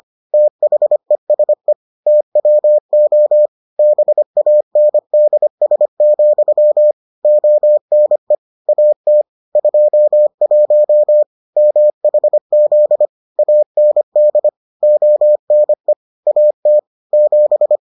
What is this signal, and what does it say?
Morse code, heard as a keyed tone, about 25 wpm, 600 Hz. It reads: THESE TWO BANDS, ONE AT 21 MHZ AND ONE AT 7